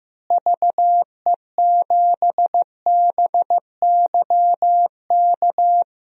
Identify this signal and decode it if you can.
Morse code, heard as a keyed tone, about 16 wpm, 700 Hz. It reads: VE7BYK
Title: DXCC-100 Call Signs - 6 Characters - Repeated 2x - 15wpm